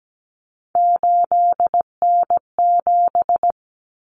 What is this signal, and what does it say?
Morse code, heard as a keyed tone, about 17 wpm, 700 Hz. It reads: 8N7
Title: Common Cut Numbers - 3-Digit 17wpm